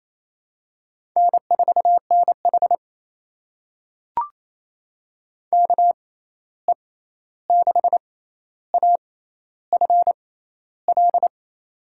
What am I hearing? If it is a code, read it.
Morse code, heard as a keyed tone, about 28 wpm, spaced as for 13 wpm, 700 Hz. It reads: N4NH KE6AFL
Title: USA Call Signs - 6x Character Spacing - 28wpm